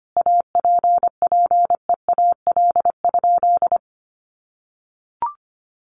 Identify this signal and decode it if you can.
Morse code, heard as a keyed tone, about 25 wpm, 700 Hz. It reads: APPEAL?
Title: Sentences from Top 1300 Words 25wpm